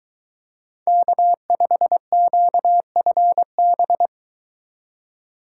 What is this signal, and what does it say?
Morse code, heard as a keyed tone, about 21 wpm, 700 Hz. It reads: K5QFB